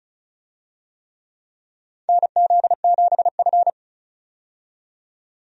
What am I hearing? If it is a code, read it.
Morse code, heard as a keyed tone, about 36 wpm, 700 Hz. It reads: NZ7F
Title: USA Call Signs - 4 Characters - 35wpm